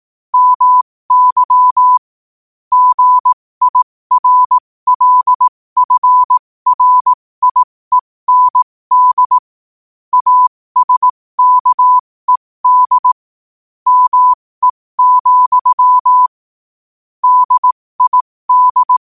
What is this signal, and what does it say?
Morse code, heard as a keyed tone, about 18 wpm, spaced as for 15 wpm, 1000 Hz. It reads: MY GIRLFRIEND ASKED ME, DID
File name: qotdfiles / 15wpm